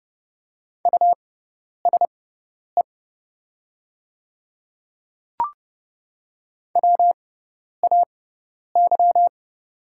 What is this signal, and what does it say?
Morse code, heard as a keyed tone, about 30 wpm, spaced as for 10 wpm, 700 Hz. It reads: USE WAY
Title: 6x Character Spacing - Top 100 Words - 30wpm